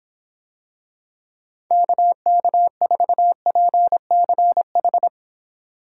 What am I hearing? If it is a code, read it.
Morse code, heard as a keyed tone, about 25 wpm, 700 Hz. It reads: KK4PCH